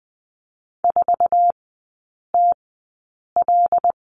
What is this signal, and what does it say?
Morse code, heard as a keyed tone, about 20 wpm, spaced as for 8 wpm, 700 Hz. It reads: 4TL
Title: 081 Lesson 14e - Number 2 - Three Characters 20wpm